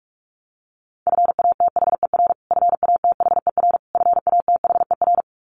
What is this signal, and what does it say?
Morse code, heard as a keyed tone, about 44 wpm, 700 Hz. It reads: FATHER FATHER FATHER
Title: Top 500 Words Repeated 45wpm